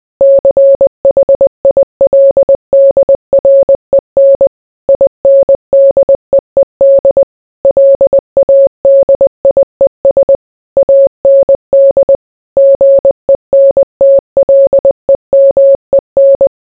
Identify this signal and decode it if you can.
Morse code, heard as a keyed tone, about 20 wpm, 550 Hz. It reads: CHILDREN INDEED LADIES AND GENTLEMEN